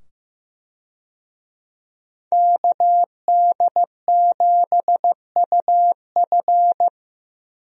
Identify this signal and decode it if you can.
Morse code, heard as a keyed tone, about 15 wpm, 700 Hz. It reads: KD7UF
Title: USA Call Signs - 5 Characters - 15wpm